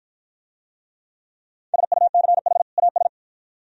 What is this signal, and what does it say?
Morse code, heard as a keyed tone, about 53 wpm, 700 Hz. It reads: SUCH AS